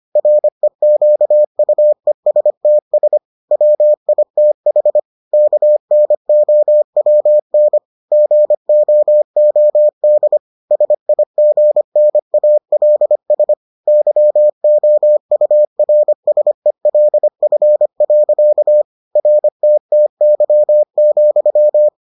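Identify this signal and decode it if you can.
Morse code, heard as a keyed tone, about 25 wpm, 600 Hz. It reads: REQUESTS WITH KNOWN GOOD SIGNALS YOURSELF. RTTY,
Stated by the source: Morse edition of the Foundations of Amateur Radio episodes.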